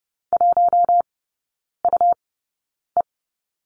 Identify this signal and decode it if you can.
Morse code, heard as a keyed tone, about 30 wpm, spaced as for 9 wpm, 690 Hz. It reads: 1UE
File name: Lesson 17e - Letter u - Three Characters 30wpm